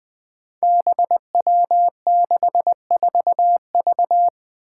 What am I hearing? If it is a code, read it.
Morse code, heard as a keyed tone, about 20 wpm, 700 Hz. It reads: BW64V
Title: Five Challenging Letter-Number Combinations 20wpm